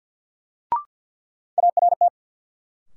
Morse code, heard as a keyed tone, about 47 wpm, 700 Hz. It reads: ART
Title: Sentences from Top 600 Words - New Word Drill - 50wpm